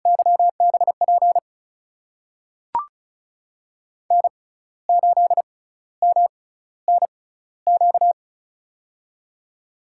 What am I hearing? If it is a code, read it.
Morse code, heard as a keyed tone, about 36 wpm, spaced as for 15 wpm, 700 Hz. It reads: YBP N8MNQ
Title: USA Call Signs - 6x Character Spacing - 35wpm